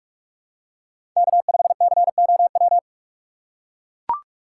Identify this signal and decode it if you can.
Morse code, heard as a keyed tone, about 44 wpm, 690 Hz. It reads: K5COW
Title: USA Call Signs - 5 Characters - 8x Character Spacing - 45wpm